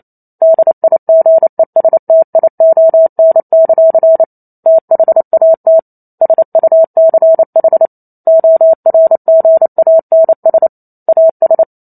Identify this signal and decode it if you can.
Morse code, heard as a keyed tone, about 28 wpm, 650 Hz. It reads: DIGESTION; THAT SUCH ORGANS AS